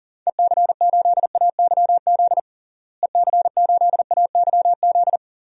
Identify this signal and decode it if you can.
Morse code, heard as a keyed tone, about 40 wpm, 700 Hz. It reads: EC8AYZ EC8AYZ
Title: DXCC-100 Call Signs - 6 Characters - Repeated 3x - 40wpm